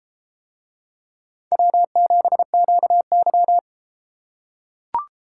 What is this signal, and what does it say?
Morse code, heard as a keyed tone, about 31 wpm, 700 Hz. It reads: W7QY